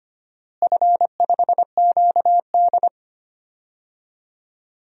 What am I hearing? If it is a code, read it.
Morse code, heard as a keyed tone, about 25 wpm, 700 Hz. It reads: F5QD
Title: DXCC-100 Call Signs - 4 Characters - Repeated 2x - 25wpm